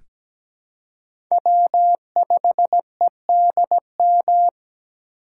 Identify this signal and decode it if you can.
Morse code, heard as a keyed tone, about 17 wpm, 700 Hz. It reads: W5EDM
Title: USA Call Signs - 5 Characters - 4x Character Spacing - 17wpm